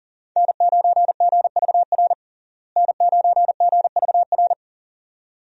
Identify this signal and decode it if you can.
Morse code, heard as a keyed tone, about 40 wpm, 700 Hz. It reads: N9GVR N9GVR
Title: USA Call Signs - 5 Characters - Repeated 2x - 40wpm